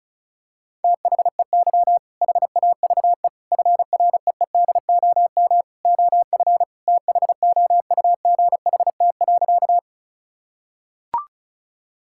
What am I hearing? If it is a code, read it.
Morse code, heard as a keyed tone, about 36 wpm, 700 Hz. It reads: THEY HAVE FREEDOM OF THOUGHT.
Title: Sentences from Top 1000 Words - Review Part 5 - 35wpm